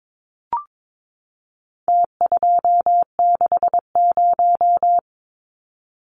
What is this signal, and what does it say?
Morse code, heard as a keyed tone, about 22 wpm, 700 Hz. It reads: T260
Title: Numbers - Contest Style - 4-Digit 22wpm